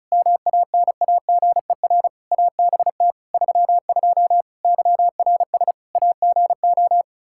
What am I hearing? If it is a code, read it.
Morse code, heard as a keyed tone, about 36 wpm, 700 Hz. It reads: MANAGER ABT 32 YRS AGO
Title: Practice QSO - Exchange 3 - More Info - 35wpm